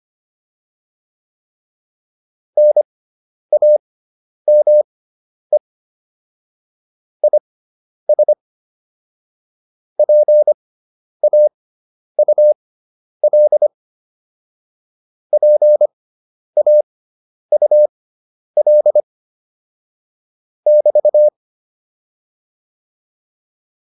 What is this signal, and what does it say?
Morse code, heard as a keyed tone, about 25 wpm, spaced as for 10 wpm, 600 Hz. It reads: NAME IS PAUL PAUL <BT>